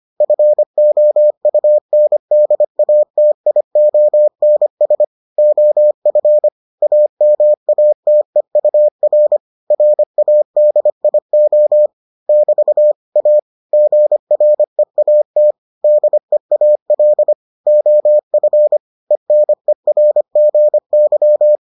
Morse code, heard as a keyed tone, about 25 wpm, 600 Hz. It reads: FOUNDATIONS OF AMATEUR RADIO <BT> A GREAT DEAL OF ENERGY